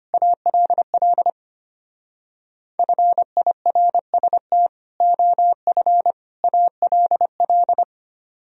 Morse code, heard as a keyed tone, about 31 wpm, 700 Hz. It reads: ALL FIRST OF ALL
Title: Speed-Racing - Sets of 3 Words 25wpm